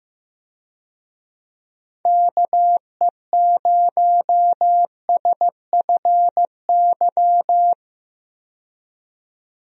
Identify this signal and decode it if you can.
Morse code, heard as a keyed tone, about 15 wpm, 700 Hz. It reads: KE0SFY
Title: USA Call Signs - 6 Characters - 15wpm